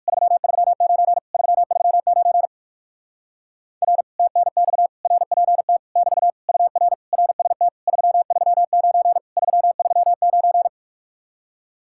Morse code, heard as a keyed tone, about 56 wpm, 700 Hz. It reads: 339 339 R TNX RPT <BT> UR RST 339 339
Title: Speed-Racing Common QSO Phrases - 45wpm